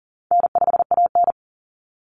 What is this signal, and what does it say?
Morse code, heard as a keyed tone, about 40 wpm, 700 Hz. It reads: N5AN
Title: Easy Call Signs Repeated 3x - 40wpm